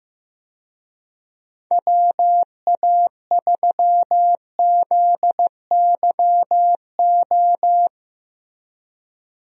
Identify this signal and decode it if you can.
Morse code, heard as a keyed tone, about 15 wpm, 700 Hz. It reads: WA3ZYO
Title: USA Call Signs - 6 Characters - 15wpm